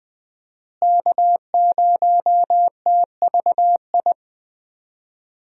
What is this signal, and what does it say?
Morse code, heard as a keyed tone, about 20 wpm, 700 Hz. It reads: K0TVI
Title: USA Call Signs - 5 Characters - Repeated 3x - 20wpm